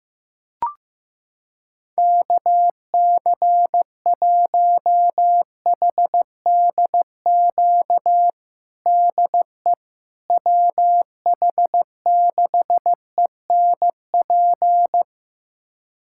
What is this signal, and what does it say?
Morse code, heard as a keyed tone, about 15 wpm, 700 Hz. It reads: KC1HDQ DE WH6ENP
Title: USA Callsign DE Callsign - 6 Characters - 15wpm